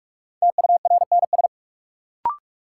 Morse code, heard as a keyed tone, about 46 wpm, 700 Hz. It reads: TURNS
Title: Sets of 5 Words Encore - New Word Drill - 45wpm